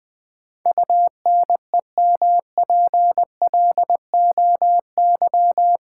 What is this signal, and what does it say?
Morse code, heard as a keyed tone, about 20 wpm, 700 Hz. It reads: UNEMPLOY
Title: Build A Word - 4 Letters At A Time - 12-Letter Words - 20wpm